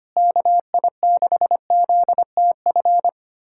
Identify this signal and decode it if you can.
Morse code, heard as a keyed tone, about 25 wpm, 700 Hz. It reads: KI6ZTF
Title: USA Callsign DE Callsign - 6 Characters - 25wpm